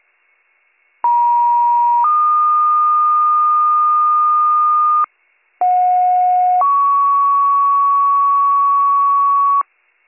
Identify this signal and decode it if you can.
Two-tone page, heard as a one-second tone, then a three-second tone, about 950 then 1220 Hz, about 730 then 1120 Hz. It is heard from 1.0 to 9.6 s.